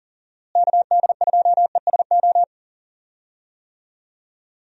Morse code, heard as a keyed tone, about 40 wpm, 700 Hz. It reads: KD2ESO
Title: USA Call Signs - 6 Characters - Repeated 3x - 40wpm